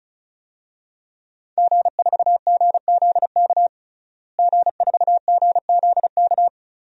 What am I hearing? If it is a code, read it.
Morse code, heard as a keyed tone, about 34 wpm, 700 Hz. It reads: G4GZK G4GZK
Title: DXCC-100 Call Signs - 5 Characters - Repeated 2x - 35wpm